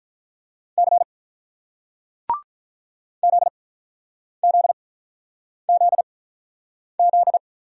Morse code, heard as a keyed tone, about 48 wpm, spaced as for 9 wpm, 700 Hz. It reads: C ZZZZ